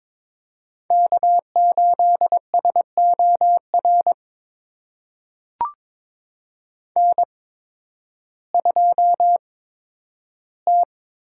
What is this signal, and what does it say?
Morse code, heard as a keyed tone, about 22 wpm, spaced as for 12 wpm, 700 Hz. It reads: K8SOR N2T
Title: USA Call Signs - 5 Characters - 8x Character Spacing - 22wpm